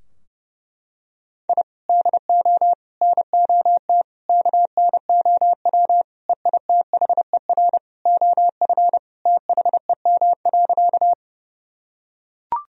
Morse code, heard as a keyed tone, about 30 wpm, 700 Hz. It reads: I DO NOT KNOW EITHER OF THEM.
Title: Sentences from Top 1000 Words - Review Part 4 - 30wpm